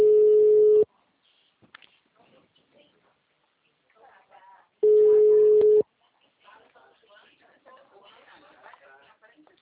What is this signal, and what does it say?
Ringback tone (Europe, 425 Hz), with speech.